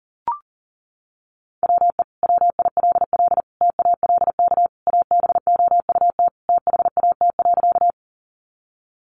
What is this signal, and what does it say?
Morse code, heard as a keyed tone, about 40 wpm, 700 Hz. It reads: WE WILL TALK ABOUT THAT.
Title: Sentences from Top 400 Words 40wpm